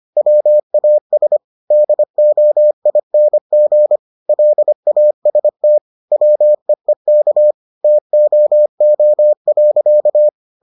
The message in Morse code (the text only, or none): WAS DOING LAST WEEK TOO.